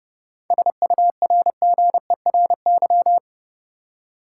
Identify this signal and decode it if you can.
Morse code, heard as a keyed tone, about 32 wpm, 700 Hz. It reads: SURGERY
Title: Seven Letter Words - 30wpm